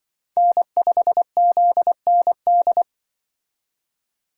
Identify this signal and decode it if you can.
Morse code, heard as a keyed tone, about 25 wpm, 700 Hz. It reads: N5ZND
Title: Speed Racing - USA 1x3 Call Signs 20wpm